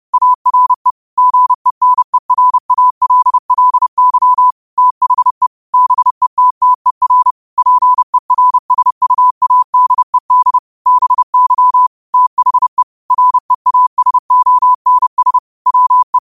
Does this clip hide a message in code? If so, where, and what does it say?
Yes: Morse code from 0.1 to 16.2 s, heard as a keyed tone, about 30 wpm, 1000 Hz. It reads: ARE GENERALLY THE BETTER PERSUADED BY THE REASONS WE